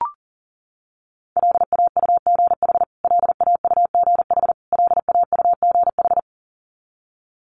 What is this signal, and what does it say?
Morse code, heard as a keyed tone, about 40 wpm, 700 Hz. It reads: LAUGH LAUGH LAUGH
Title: Top 500 Words Repeated 40wpm